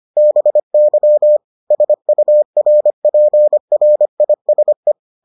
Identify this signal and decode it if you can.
Morse code, heard as a keyed tone, about 25 wpm, 600 Hz. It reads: BY SURPRISE